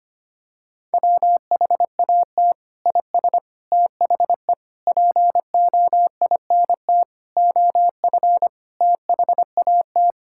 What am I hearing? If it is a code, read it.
Morse code, heard as a keyed tone, about 25 wpm, 700 Hz. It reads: WHAT IS THE POINT OF THAT
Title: Sentences from Top 1000 Words - Review Part 5 - 25wpm